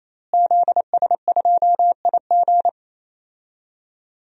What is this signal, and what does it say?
Morse code, heard as a keyed tone, about 28 wpm, 700 Hz. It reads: ZS2IG
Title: DXCC-100 Call Signs - 5 Characters - Repeated 3x - 28wpm